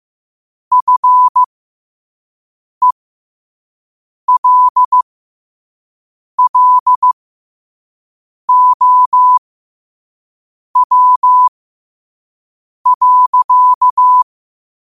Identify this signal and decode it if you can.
Morse code, heard as a keyed tone, about 15 wpm, spaced as for 5 wpm, 1000 Hz. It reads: FELLOW.
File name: qotdfiles / 5wpm